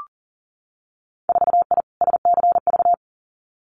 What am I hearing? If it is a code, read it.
Morse code, heard as a keyed tone, about 38 wpm, 700 Hz. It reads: 4I SCV
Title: Winter Field Day Exchanges Encore 40wpm